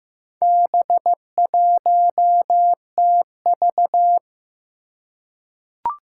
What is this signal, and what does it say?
Morse code, heard as a keyed tone, about 15 wpm, 700 Hz. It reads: B1TV
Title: Four Challenging Letter-Number Combinations 15wpm